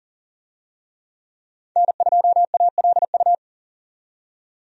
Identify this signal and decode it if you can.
Morse code, heard as a keyed tone, about 40 wpm, 700 Hz. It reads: N2ALU
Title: USA Call Signs - 5 Characters - 40wpm